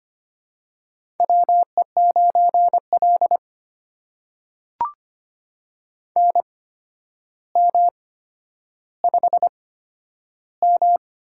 Morse code, heard as a keyed tone, about 25 wpm, spaced as for 11 wpm, 700 Hz. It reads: WE9L NM5M
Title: USA Call Signs - 4 Characters - 8x Character Spacing - 25wpm